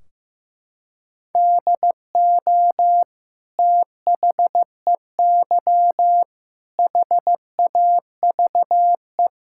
Morse code, heard as a keyed tone, about 15 wpm, 700 Hz. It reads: DO THEY HAVE
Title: Sentences from Top 1600 Words 15wpm